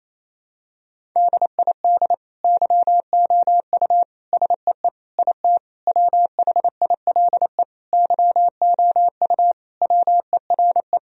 Morse code, heard as a keyed tone, about 28 wpm, 700 Hz. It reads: DID YOU SEE IT WHILE YOU WERE
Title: Sentences from Top 1000 Words - Review Part 1 - 28wpm